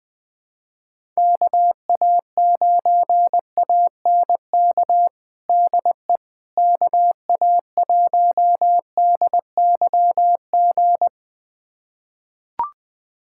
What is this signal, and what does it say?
Morse code, heard as a keyed tone, about 20 wpm, 700 Hz. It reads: KA9ANK DE KA1DYG
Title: USA Callsign DE Callsign - 6 Characters - 20wpm